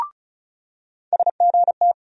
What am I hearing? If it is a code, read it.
Morse code, heard as a keyed tone, about 36 wpm, 700 Hz. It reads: SGT
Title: USA Call Sign Postfixes - 3 Letters 35wpm